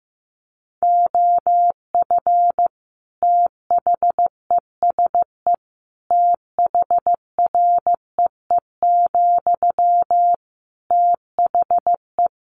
Morse code, heard as a keyed tone, about 15 wpm, 700 Hz. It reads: OF THESE THREE, THE
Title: Sentences from Top 700 Words 15wpm